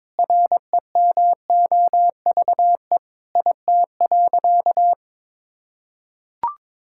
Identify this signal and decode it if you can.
Morse code, heard as a keyed tone, about 22 wpm, 700 Hz. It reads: REMOVE IT.
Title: Sentences from Top 1000 Words - Review Part 5 - 22wpm